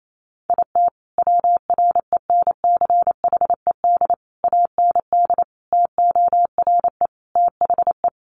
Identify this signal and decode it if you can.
Morse code, heard as a keyed tone, about 28 wpm, 700 Hz. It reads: IT WRENCHED AND TORE THE
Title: A Sound Of Thunder 28wpm